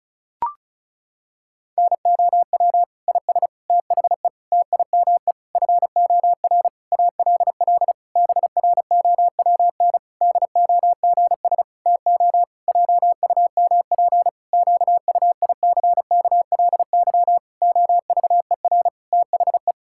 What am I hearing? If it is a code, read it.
Morse code, heard as a keyed tone, about 34 wpm, 700 Hz. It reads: NOW IS THE TIME FOR ALL BROWN DOGS TO JUMP QUICKLY OVER THE
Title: Pangram Challenge v2 - 35wpm